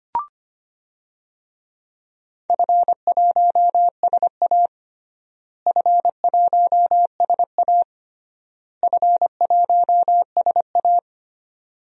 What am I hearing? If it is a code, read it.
Morse code, heard as a keyed tone, about 25 wpm, 700 Hz. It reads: F1SA F1SA F1SA
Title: DXCC-100 Call Signs - 4 Characters - Repeated 3x - 25wpm